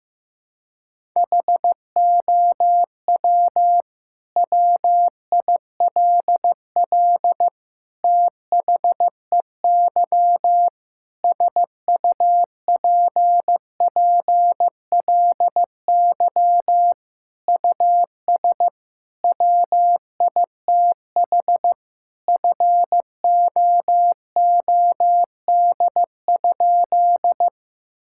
Morse code, heard as a keyed tone, about 15 wpm, 700 Hz. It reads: HOW WILL THEY SUPPLY US WITH FOOD?
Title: Sentences from Top 1000 Words - Review Part 5 - 15wpm